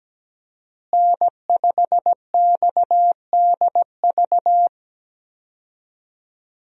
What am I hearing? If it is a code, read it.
Morse code, heard as a keyed tone, about 17 wpm, 700 Hz. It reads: N5XDV
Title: USA Call Signs - 5 Characters - Repeated 2x - 17wpm